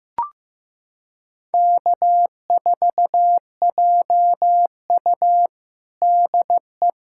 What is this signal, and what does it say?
Morse code, heard as a keyed tone, about 15 wpm, 700 Hz. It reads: K4JU DE
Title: USA Callsign DE Callsign - 4 Characters - 15wpm